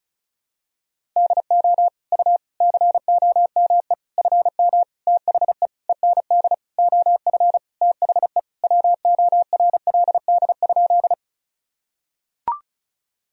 Morse code, heard as a keyed tone, about 34 wpm, 700 Hz. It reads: DO U COME FM THE END OF THE WORLD?
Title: Sentences from Top 200 Words and Abbreviations 35wpm